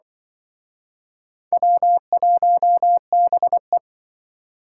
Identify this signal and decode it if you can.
Morse code, heard as a keyed tone, about 26 wpm, 700 Hz. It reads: W1BE